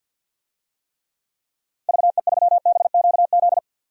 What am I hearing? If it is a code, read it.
Morse code, heard as a keyed tone, about 52 wpm, 700 Hz. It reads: VE3BQZ
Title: DXCC-100 Call Signs - 6 Characters - Repeated 3x - 50wpm